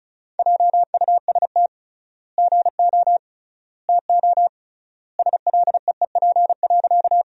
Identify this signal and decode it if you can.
Morse code, heard as a keyed tone, about 36 wpm, 700 Hz. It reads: JUST GO TO SLEEP.
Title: 3x Word-Spacing - Sentences from Top 500 Words - 35wpm